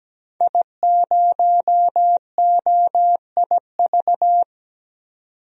Morse code, heard as a keyed tone, about 17 wpm, 700 Hz. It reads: I0OIV
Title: DXCC-100 Call Signs - 5 Characters - 17wpm